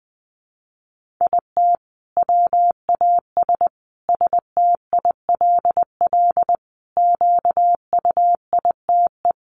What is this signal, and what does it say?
Morse code, heard as a keyed tone, about 20 wpm, 700 Hz. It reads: IT WAS STILL QUITE
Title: Sentences from Top 1500 Words 20wpm